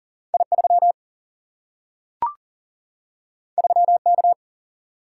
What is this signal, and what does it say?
Morse code, heard as a keyed tone, about 39 wpm, 700 Hz. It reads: I 3 3 K
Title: Single-Letter Single-Number 40wpm